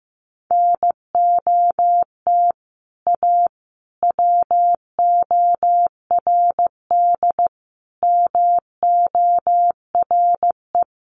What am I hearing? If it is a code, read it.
Morse code, heard as a keyed tone, about 15 wpm, 700 Hz. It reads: NOT A WORD MORE
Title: Sentences from Top 800 Words 15wpm